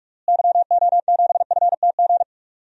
Morse code, heard as a keyed tone, about 46 wpm, 700 Hz. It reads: YO7FTG
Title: DXCC-100 Call Signs - 6 Characters - Repeated 3x - 45wpm